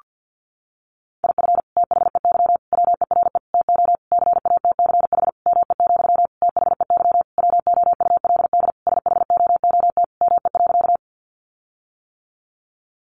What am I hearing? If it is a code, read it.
Morse code, heard as a keyed tone, about 50 wpm, 700 Hz. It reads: IF THEY WERE TO CATCH ME, THEY WOULD SHOOT ME.
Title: Sentences from Top 1500 Words 50wpm